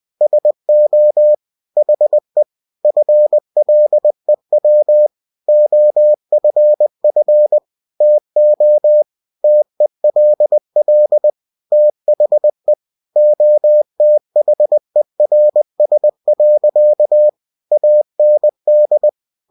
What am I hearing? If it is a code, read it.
Morse code, heard as a keyed tone, about 20 wpm, 600 Hz. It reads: SO HE FLEW OFF TO TELL THE OTHERS. AND
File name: cw-books / winnie-the-pooh / winnie-20wpm-page-0102